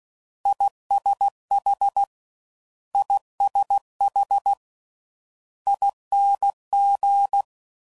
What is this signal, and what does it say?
Morse code, heard as a keyed tone, about 16 wpm, 800 Hz. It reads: ISH ISH ING